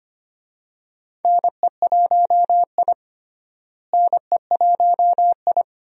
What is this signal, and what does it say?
Morse code, heard as a keyed tone, about 25 wpm, 700 Hz. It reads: NE1I NE1I
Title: USA Call Signs - 4 Characters - Repeated 2x - 25wpm